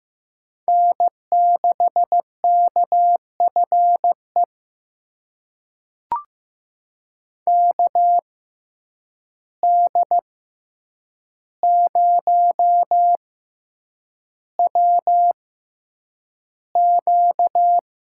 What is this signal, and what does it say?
Morse code, heard as a keyed tone, about 15 wpm, spaced as for 8 wpm, 700 Hz. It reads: N6KFE KD0WQ
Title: USA Call Signs - 6x Character Spacing - 15wpm